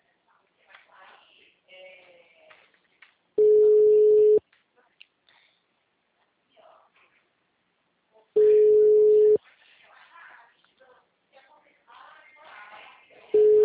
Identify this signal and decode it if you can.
Ringback tone (Europe, 425 Hz), with speech.